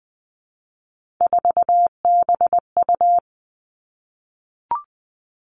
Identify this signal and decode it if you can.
Morse code, heard as a keyed tone, about 20 wpm, 700 Hz. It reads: 4BU
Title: Three Challenging Letter-Number Combinations 20wpm